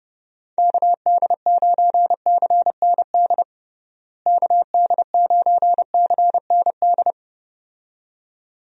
Morse code, heard as a keyed tone, about 30 wpm, 700 Hz. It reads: KD9CND KD9CND
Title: USA Call Signs - 6 Characters - Repeated 2x - 30wpm